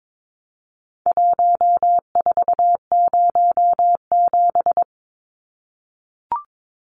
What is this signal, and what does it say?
Morse code, heard as a keyed tone, about 22 wpm, 700 Hz. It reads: 1407
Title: Numbers - Contest Style - 4-Digit 22wpm